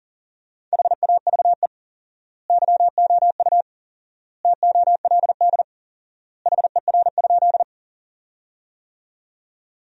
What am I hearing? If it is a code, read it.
Morse code, heard as a keyed tone, about 40 wpm, 700 Hz. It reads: HAVE YOU TOLD HER?
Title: 4x Word-Spacing - Sentences from Top 400 Words - 40wpm